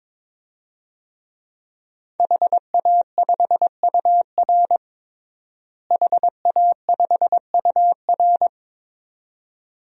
Morse code, heard as a keyed tone, about 22 wpm, 700 Hz. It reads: HA5UR HA5UR
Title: DXCC-100 Call Signs - 5 Characters - Repeated 2x - 22wpm